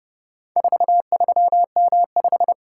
Morse code, heard as a keyed tone, about 29 wpm, 700 Hz. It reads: 43M5